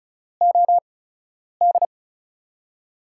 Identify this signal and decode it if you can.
Morse code, heard as a keyed tone, about 36 wpm, spaced as for 10 wpm, 700 Hz. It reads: OD
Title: USA Call Signs - 6 Characters - 8x Character Spacing - 35wpm